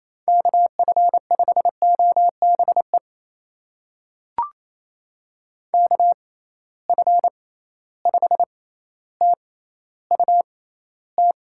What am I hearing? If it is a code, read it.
Morse code, heard as a keyed tone, about 28 wpm, spaced as for 14 wpm, 700 Hz. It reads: KF5OBE KF5TUT
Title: USA Call Signs - 6 Characters - 6x Character Spacing - 28wpm